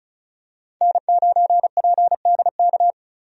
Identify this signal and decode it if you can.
Morse code, heard as a keyed tone, about 36 wpm, 700 Hz. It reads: N9PDK
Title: USA Call Signs - 5 Characters - 35wpm